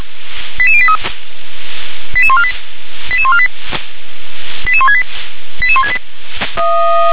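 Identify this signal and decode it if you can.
Five-tone selective call (ZVEI), heard as quick runs of five tones, tones of about 70 ms. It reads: 88083 88148 88148 88168 88168
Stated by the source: ZVEI five-tone call received on SDR.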